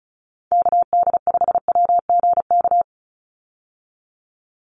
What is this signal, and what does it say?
Morse code, heard as a keyed tone, about 34 wpm, 700 Hz. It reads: KD5WGK
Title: USA Call Signs 35wpm